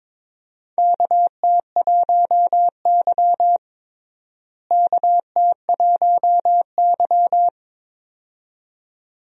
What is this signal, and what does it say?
Morse code, heard as a keyed tone, about 22 wpm, 700 Hz. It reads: KT1Y KT1Y
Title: USA Call Signs - 4 Characters - Repeated 2x - 22wpm